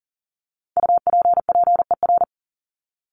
Morse code, heard as a keyed tone, about 40 wpm, 700 Hz. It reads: UPPER
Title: Top 1751-2000 Words 40wpm